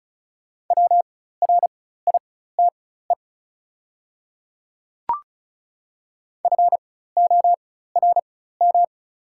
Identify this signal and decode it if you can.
Morse code, heard as a keyed tone, about 36 wpm, spaced as for 16 wpm, 690 Hz. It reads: WRITE FORM
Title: 4x Character Spacing - Top 200 Words - 35wpm